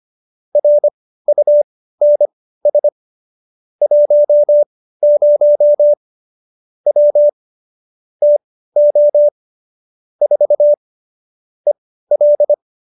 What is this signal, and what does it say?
Morse code, heard as a keyed tone, about 25 wpm, spaced as for 15 wpm, 590 Hz. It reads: RUNS 10 W TO 4 EL